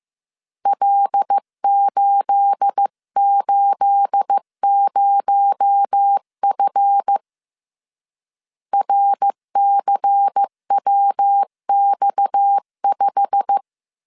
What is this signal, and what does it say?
Morse code, heard as a keyed tone, about 15 wpm, 800 Hz. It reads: L880F RCWX5
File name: CW / DET / 1 9 1500C S C 14wpm 3245